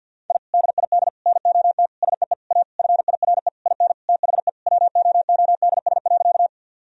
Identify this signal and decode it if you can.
Morse code, heard as a keyed tone, about 53 wpm, 700 Hz. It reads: I DID NOT SEE A FIRE IN THE WOODS.
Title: Sentences from Top 1000 Words - Review Part 2 - 50wpm